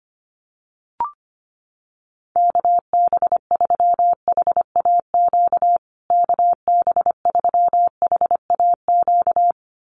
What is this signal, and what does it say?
Morse code, heard as a keyed tone, about 25 wpm, 700 Hz. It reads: KB3HAQ KB3HAQ
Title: USA Call Signs Repeated 2x 25wpm